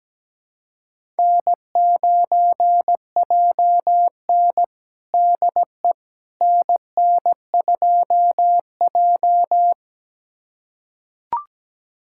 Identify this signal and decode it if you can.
Morse code, heard as a keyed tone, about 17 wpm, 700 Hz. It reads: N9JN DE NN2J
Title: USA Callsign DE Callsign - 4 Characters - 17wpm